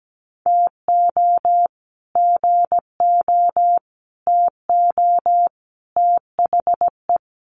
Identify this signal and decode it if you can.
Morse code, heard as a keyed tone, about 17 wpm, 700 Hz. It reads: TO GO TO THE
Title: Sets of 4 Words 17wpm